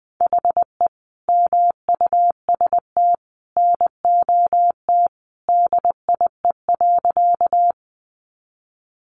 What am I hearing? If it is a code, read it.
Morse code, heard as a keyed tone, about 19 wpm, 700 Hz. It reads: HE MUST NOT DIE.
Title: Sentences from Top 1300 Words 20wpm